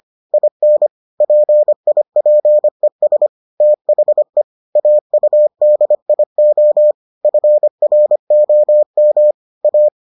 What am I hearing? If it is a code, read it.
Morse code, heard as a keyed tone, about 25 wpm, 600 Hz. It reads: IN PIPES THE AUDIO FROM A